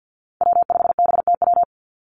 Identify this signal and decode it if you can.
Morse code, heard as a keyed tone, about 50 wpm, 700 Hz. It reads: W5BTW
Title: USA Call Signs 50wpm